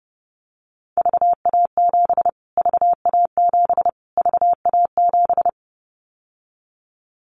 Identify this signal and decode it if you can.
Morse code, heard as a keyed tone, about 30 wpm, 700 Hz. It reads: VA7 VA7 VA7
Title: Callsign Prefixes for Top 25 DXCC - Repeated 3x - 30wpm